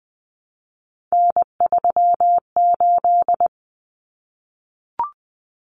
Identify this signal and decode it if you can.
Morse code, heard as a keyed tone, about 20 wpm, 700 Hz. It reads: N38
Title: Common Cut Numbers - 3-Digit 20wpm